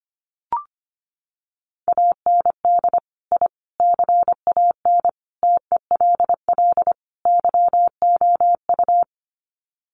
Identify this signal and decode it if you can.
Morse code, heard as a keyed tone, about 25 wpm, 700 Hz. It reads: AND I CAN TELL YOU
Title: Sets of 5 Words Encore2 25wpm